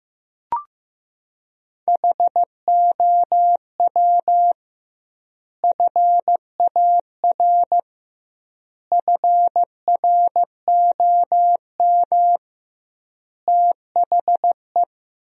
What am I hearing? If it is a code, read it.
Morse code, heard as a keyed tone, about 15 wpm, 700 Hz. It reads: HOW FAR FROM THE
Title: 2x Word-Spacing - Sentences from Top 400 Words - 15wpm